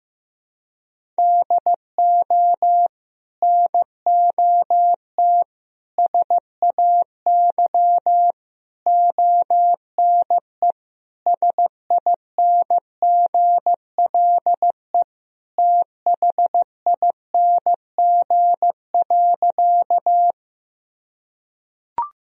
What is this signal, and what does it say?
Morse code, heard as a keyed tone, about 16 wpm, 700 Hz. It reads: DO NOT SAY ONE SINGLE THING.
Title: Sentences from Top 1000 Words - Review Part 4 - 15wpm